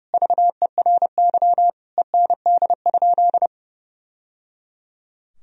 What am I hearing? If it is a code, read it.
Morse code, heard as a keyed tone, about 30 wpm, 700 Hz. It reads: VERY END?
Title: Sentences from Top 1000 Words - Review Part 3 - 30wpm